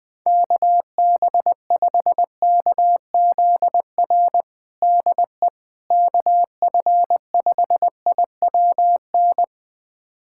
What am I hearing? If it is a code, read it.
Morse code, heard as a keyed tone, about 20 wpm, 700 Hz. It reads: KB5KZR DE KF5IWN
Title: USA Callsign DE Callsign - 6 Characters - 20wpm